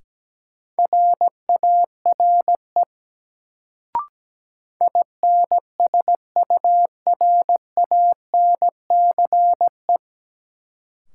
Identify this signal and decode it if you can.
Morse code, heard as a keyed tone, about 17 wpm, 700 Hz. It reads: RARE INSURANCE
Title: Sentences from Top 1100 Words - New Word Drill - 17wpm